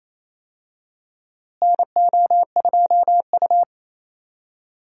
Morse code, heard as a keyed tone, about 28 wpm, 700 Hz. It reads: NO2U
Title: USA Call Signs - 4 Characters - Repeated 3x - 28wpm